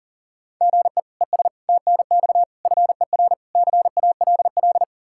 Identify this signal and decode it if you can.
Morse code, heard as a keyed tone, about 37 wpm, 700 Hz. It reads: GE ES TNX FER CALL
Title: K1USN SST QSO Practice 40wpm